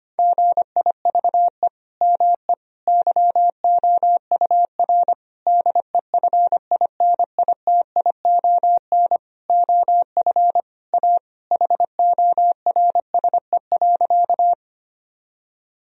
Morse code, heard as a keyed tone, about 25 wpm, 700 Hz. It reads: GIVE ME YOUR DEFINITION OF A HORSE.
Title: Sentences from Top 1000 Words - Review Part 4 - 25wpm